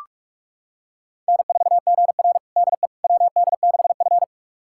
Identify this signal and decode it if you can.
Morse code, heard as a keyed tone, about 44 wpm, 690 Hz. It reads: N4GR DE WD6F
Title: USA Callsign DE Callsign - 4 Characters - 45wpm